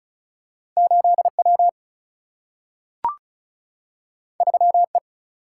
Morse code, heard as a keyed tone, about 36 wpm, 700 Hz. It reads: 8 W 3 E
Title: Single-Letter Single-Number 35wpm